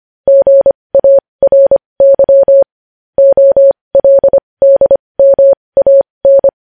Morse code, heard as a keyed tone, about 25 wpm, spaced as for 20 wpm, 550 Hz. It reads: GARY OLDMAN
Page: News headlines in Morse code at 20 WPM